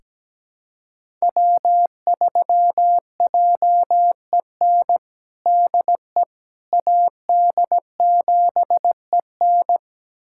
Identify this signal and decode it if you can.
Morse code, heard as a keyed tone, about 17 wpm, 700 Hz. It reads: W3JEN DE AD7EN
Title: USA Callsign DE Callsign - 5 Characters - 17wpm